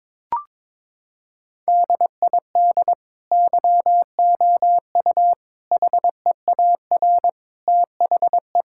Morse code, heard as a keyed tone, about 22 wpm, 700 Hz. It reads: DID YOU HEAR THE
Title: Sentences from Top 1000 Words - Review Part 5 - 22wpm